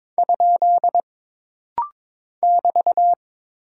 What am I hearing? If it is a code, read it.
Morse code, heard as a keyed tone, about 22 wpm, 700 Hz. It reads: ? <BT>
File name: practice / 22wpm / Single Punctuation-Prosign - Rapid-Fire - 22wpm